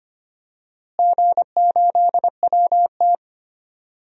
Morse code, heard as a keyed tone, about 25 wpm, 700 Hz. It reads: G8WT
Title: DXCC-100 Call Signs - 4 Characters - Repeated 3x - 25wpm